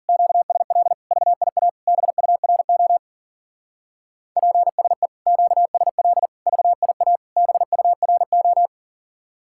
Morse code, heard as a keyed tone, about 46 wpm, 700 Hz. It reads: QSL VIA BURO PSE QSL VIA BURO
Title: Speed-Racing Common QSO Phrases - 40wpm